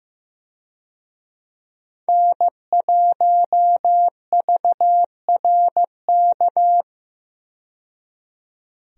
Morse code, heard as a keyed tone, about 15 wpm, 700 Hz. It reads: N1VRK
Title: USA Call Signs - 5 Characters - 15wpm